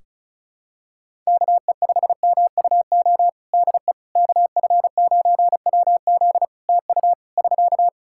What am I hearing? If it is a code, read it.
Morse code, heard as a keyed tone, about 34 wpm, 700 Hz. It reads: KE5MUO DE KF9WZ TU <SK>
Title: K1USN SST QSO Practice 35wpm